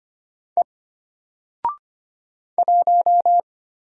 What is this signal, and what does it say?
Morse code, heard as a keyed tone, about 25 wpm, 700 Hz. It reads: E 1
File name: Single-Letter-Number-25wpm